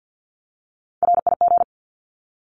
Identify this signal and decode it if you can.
Morse code, heard as a keyed tone, about 50 wpm, 700 Hz. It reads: RIG
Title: Abbreviations Repeated 3x - 50wpm